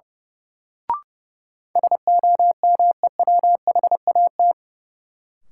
Morse code, heard as a keyed tone, about 32 wpm, 700 Hz. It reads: SOMEWHAT
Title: Sentences from Top 1000 Words - Review Part 1 - New Word Drill - 30wpm